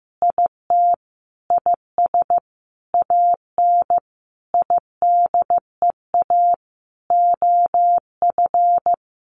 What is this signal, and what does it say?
Morse code, heard as a keyed tone, about 15 wpm, 700 Hz. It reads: IT IS AN IDEA OF
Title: Sentences from Top 1500 Words 15wpm